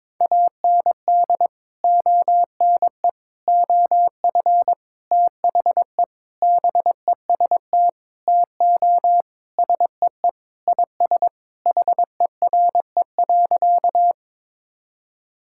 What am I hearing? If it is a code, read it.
Morse code, heard as a keyed tone, about 22 wpm, 700 Hz. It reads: AND ONE OF THE BEST TO SEE IS HERE.
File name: practice / 22wpm / Sentences with 4-Word Phrases - 22wpm